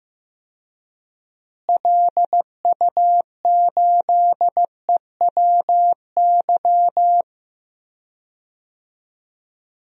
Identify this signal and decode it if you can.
Morse code, heard as a keyed tone, about 15 wpm, 700 Hz. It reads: LU8EWY
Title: DXCC-100 Call Signs - 6 Characters - Repeated 2x - 15wpm